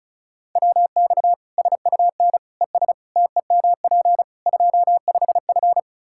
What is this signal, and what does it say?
Morse code, heard as a keyed tone, about 36 wpm, 700 Hz. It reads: WX SUN ES TEMP 25F
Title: Practice QSO - Exchange 2 - Info - 35wpm